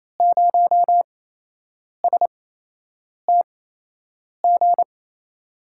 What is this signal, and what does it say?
Morse code, heard as a keyed tone, about 28 wpm, spaced as for 8 wpm, 700 Hz. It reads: 0STG
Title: USA Call Signs - 8x Character Spacing - 28wpm